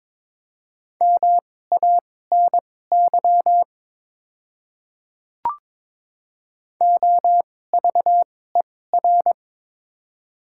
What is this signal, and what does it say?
Morse code, heard as a keyed tone, about 22 wpm, spaced as for 16 wpm, 690 Hz. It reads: MANY OVER
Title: 2x Character Spacing - Top 100 Words - 22wpm